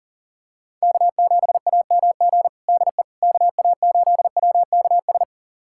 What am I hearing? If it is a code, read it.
Morse code, heard as a keyed tone, about 37 wpm, 700 Hz. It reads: K7AMG DE KA8WKS
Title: USA Callsign DE Callsign 40wpm